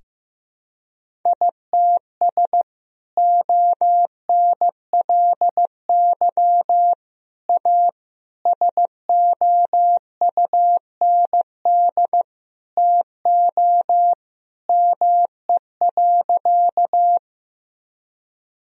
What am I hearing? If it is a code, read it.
Morse code, heard as a keyed tone, about 15 wpm, 700 Hz. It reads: ITS ONLY A SOUND TO ME.
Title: Sentences from Top 1000 Words - Review Part 1 - 15wpm